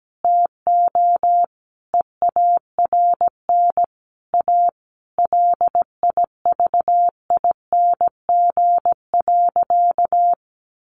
Morse code, heard as a keyed tone, about 17 wpm, 700 Hz. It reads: TO EARN A LIVING.
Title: Sentences from Top 1500 Words 17wpm